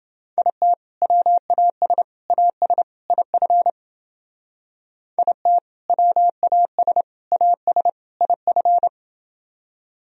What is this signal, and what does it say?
Morse code, heard as a keyed tone, about 29 wpm, 700 Hz. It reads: IT WAS AS IF IT WAS AS IF